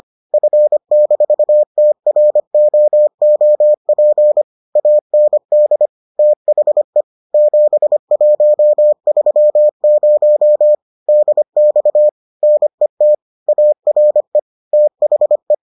Morse code, heard as a keyed tone, about 25 wpm, 600 Hz. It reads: F-TROOP AND THE 7130 DX NET ARE THE